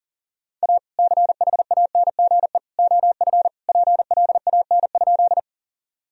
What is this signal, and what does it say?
Morse code, heard as a keyed tone, about 40 wpm, 700 Hz. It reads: A CHANGE OF PLAN?
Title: Sentences from Top 1000 Words - Review Part 2 - 40wpm